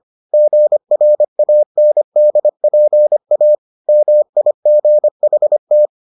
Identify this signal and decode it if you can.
Morse code, heard as a keyed tone, about 25 wpm, 600 Hz. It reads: GRANDPA MIGHT